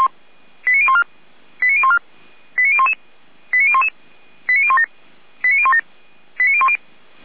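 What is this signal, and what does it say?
Five-tone selective call (ZVEI), heard as quick runs of five tones, tones of about 70 ms. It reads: 89014 89014 89011 89011 89017 89017 89010
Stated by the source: ZVEI five-tone call 89060 received by SDR.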